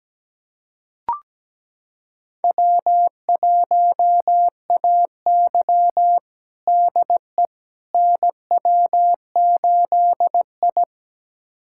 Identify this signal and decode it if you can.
Morse code, heard as a keyed tone, about 17 wpm, 700 Hz. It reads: W1AY DE NW8I
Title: USA Callsign DE Callsign - 4 Characters - 17wpm